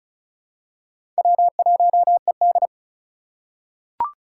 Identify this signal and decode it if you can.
Morse code, heard as a keyed tone, about 34 wpm, 700 Hz. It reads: W1ED
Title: USA Call Signs - 4 Characters - 4x Character Spacing - 35wpm